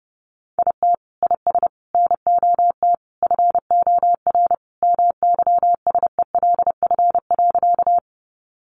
Morse code, heard as a keyed tone, about 30 wpm, 700 Hz. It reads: IT IS NOT FOR MYSELF.
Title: Sentences from Top 1300 Words 30wpm